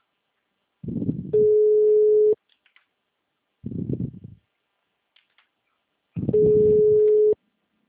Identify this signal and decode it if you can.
Ringback tone (Europe, 425 Hz).